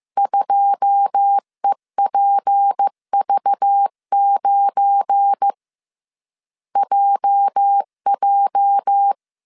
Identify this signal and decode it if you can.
Morse code, heard as a keyed tone, about 15 wpm, 800 Hz. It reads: 2EPV9 JJ
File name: CW / DET / 1 9 1500C S C 14wpm 3245